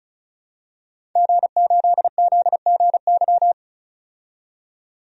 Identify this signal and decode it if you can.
Morse code, heard as a keyed tone, about 36 wpm, 700 Hz. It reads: G8ZGY
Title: DXCC-100 Call Signs - 5 Characters - 35wpm